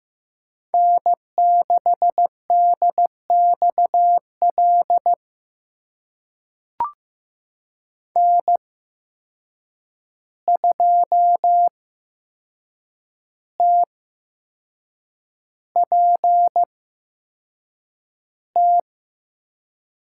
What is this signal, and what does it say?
Morse code, heard as a keyed tone, about 15 wpm, spaced as for 6 wpm, 700 Hz. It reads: N6DXL N2TPT
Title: USA Call Signs - 5 Characters - 8x Character Spacing - 15wpm